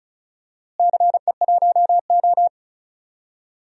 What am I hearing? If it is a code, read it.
Morse code, heard as a keyed tone, about 34 wpm, 700 Hz. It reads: CE1O
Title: DXCC-100 Call Signs - 4 Characters - Repeated 3x - 35wpm